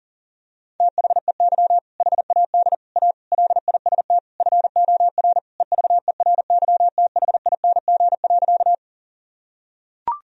Morse code, heard as a keyed tone, about 40 wpm, 700 Hz. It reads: THEY HAD A LIST FOR EVERYTHING.
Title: Sentences from Top 1000 Words - Review Part 2 - 40wpm